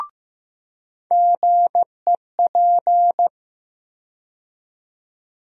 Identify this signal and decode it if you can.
Morse code, heard as a keyed tone, about 15 wpm, 700 Hz. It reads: GEP
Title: USA Call Sign Postfixes - 3 Letters 15wpm